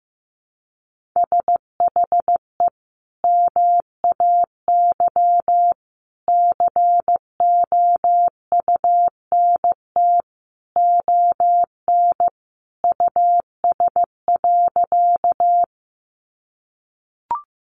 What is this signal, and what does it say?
Morse code, heard as a keyed tone, about 15 wpm, 700 Hz. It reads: SHE MAY COUNT ON US.
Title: Sentences from Top 1500 Words 15wpm